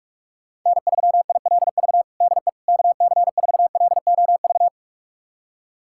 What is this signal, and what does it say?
Morse code, heard as a keyed tone, about 41 wpm, 700 Hz. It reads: N3ILV DE KC4LOV
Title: USA Callsign DE Callsign 45wpm